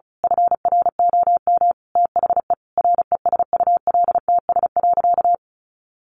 Morse code, heard as a keyed tone, about 34 wpm, 700 Hz. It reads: FROM THE RESULTS.
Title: A Sound Of Thunder 35wpm